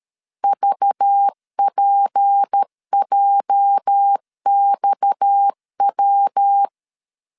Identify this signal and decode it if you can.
Morse code, heard as a keyed tone, about 13 wpm, 800 Hz. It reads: VPJXW
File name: CW / DET / 1 9 1500C S C 12wpm 3811